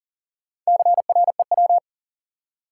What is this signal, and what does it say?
Morse code, heard as a keyed tone, about 40 wpm, 700 Hz. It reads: CREW
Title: Four Letter Words - 40wpm